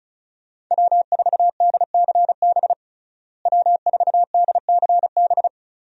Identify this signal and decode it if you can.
Morse code, heard as a keyed tone, about 34 wpm, 700 Hz. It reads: W4DCB W4DCB
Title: USA Call Signs - 5 Characters - Repeated 3x - 35wpm